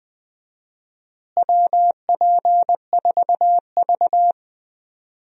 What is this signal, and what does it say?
Morse code, heard as a keyed tone, about 20 wpm, 700 Hz. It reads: WP4V
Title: USA Call Signs - 4 Characters - Repeated 3x - 20wpm